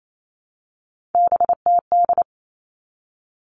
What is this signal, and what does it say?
Morse code, heard as a keyed tone, about 28 wpm, 700 Hz. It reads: BTD
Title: All Cut Numbers - 3-Digit 28wpm